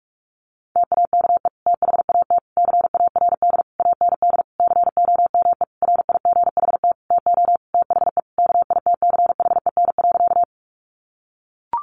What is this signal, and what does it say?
Morse code, heard as a keyed tone, about 44 wpm, 700 Hz. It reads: TAKE THAT CARD AND COME RIGHT TO THE KITCHEN.
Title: Sentences from Top 1400 Words 45wpm